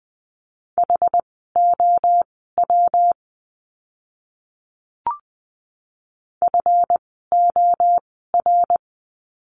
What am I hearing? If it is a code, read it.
Morse code, heard as a keyed tone, about 20 wpm, spaced as for 14 wpm, 700 Hz. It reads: HOW FOR
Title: 237 Lesson 38f - Letter q - Words 20wpm